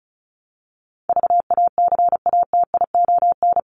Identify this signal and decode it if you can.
Morse code, heard as a keyed tone, about 34 wpm, 700 Hz. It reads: VACATION
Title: Top 1501-1750 Words 35wpm